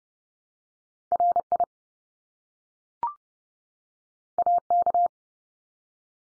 Morse code, heard as a keyed tone, about 30 wpm, 700 Hz. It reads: RI AK
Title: US States 30WPM